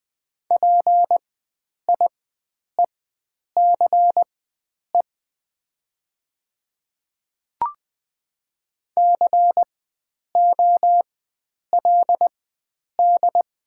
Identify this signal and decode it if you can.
Morse code, heard as a keyed tone, about 20 wpm, spaced as for 9 wpm, 690 Hz. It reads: PIECE COLD
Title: 4x Character Spacing - Top 500 Words - 20wpm